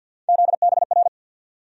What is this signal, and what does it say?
Morse code, heard as a keyed tone, about 52 wpm, 700 Hz. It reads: ZBR
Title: USA Call Sign Postfixes - 3 Letters 50wpm